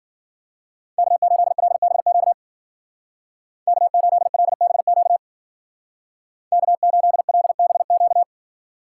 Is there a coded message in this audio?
Yes: Morse code from 1.0 to 8.2 s, heard as a keyed tone, about 53 wpm, 700 Hz. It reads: K8LBQ K8LBQ K8LBQ